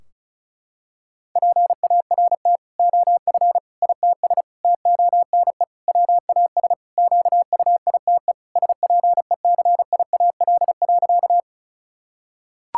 Morse code, heard as a keyed tone, about 34 wpm, 700 Hz. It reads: PART OF ITS TONE WAS QUITE SPECIAL.
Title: Sentences from Top 1600 Words 35wpm